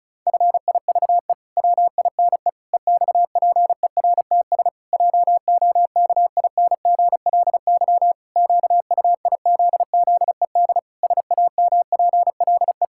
Morse code, heard as a keyed tone, about 36 wpm, 700 Hz. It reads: FIVE WINE EXPERTS JOKINGLY QUIZZED SAMPLE
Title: Pangram Challenge v2 - 35wpm